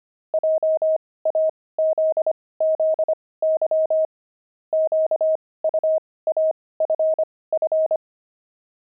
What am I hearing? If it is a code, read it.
Morse code, heard as a keyed tone, about 25 wpm, spaced as for 18 wpm, 630 Hz. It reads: JAZZY QUAFF
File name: pangram-fs-18-cs-25-saxophones